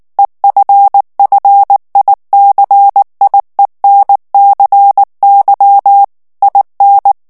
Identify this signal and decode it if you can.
Morse code, heard as a keyed tone, about 19 wpm, 800 Hz. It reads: EFFICIENCY IN
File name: ANGLAIS texte clair 3 vitesse 18 mots minute